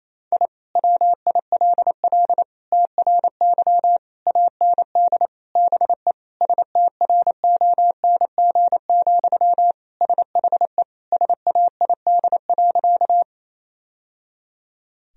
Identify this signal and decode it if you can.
Morse code, heard as a keyed tone, about 28 wpm, 700 Hz. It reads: I WILL TRY AND BE STRONG, SHE SAID.
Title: Sentences from Top 1000 Words - Review Part 3 - 28wpm